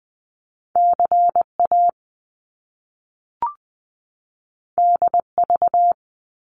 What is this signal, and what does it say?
Morse code, heard as a keyed tone, about 20 wpm, 700 Hz. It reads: CA DV
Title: Top Two Letter Combinations in Words 20wpm